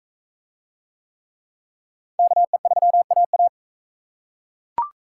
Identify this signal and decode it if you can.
Morse code, heard as a keyed tone, about 41 wpm, 700 Hz. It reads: KE3AA